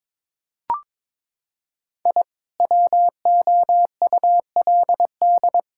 Morse code, heard as a keyed tone, about 22 wpm, 700 Hz. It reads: I WOULD
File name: practice / 22wpm / Sentences with 2-Word Phrases - 22wpm